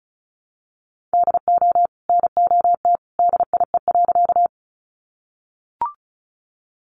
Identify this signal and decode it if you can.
Morse code, heard as a keyed tone, about 34 wpm, 700 Hz. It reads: DO NOT DIE.
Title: Sentences from Top 1300 Words 35wpm